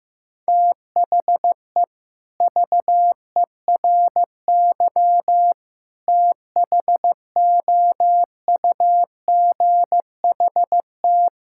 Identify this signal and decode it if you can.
Morse code, heard as a keyed tone, about 16 wpm, 690 Hz. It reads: THE VERY THOUGHT
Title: Sentences from Top 1000 Words - Review Part 5 - 15wpm